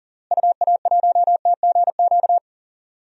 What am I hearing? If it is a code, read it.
Morse code, heard as a keyed tone, about 40 wpm, 700 Hz. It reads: UA1TGQ
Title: DXCC-100 Call Signs - 6 Characters - Repeated 2x - 40wpm